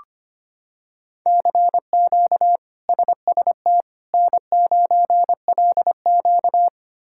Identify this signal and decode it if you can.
Morse code, heard as a keyed tone, about 25 wpm, 700 Hz. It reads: CQ SST N9LQ
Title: K1USN SST QSO Practice - 25wpm